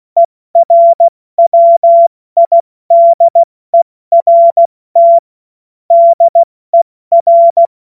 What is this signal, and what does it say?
Morse code, heard as a keyed tone, about 16 wpm, spaced as for 14 wpm, 680 Hz. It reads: ERWIDERT DER
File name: Morsekurs / Witze / Witz 07.November 2023 70 F80